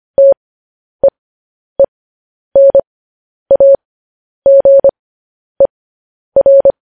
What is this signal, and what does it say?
Morse code, heard as a keyed tone, about 25 wpm, spaced as for 10 wpm, 550 Hz. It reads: TEENAGER
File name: podcast / News headlines-1x-25x10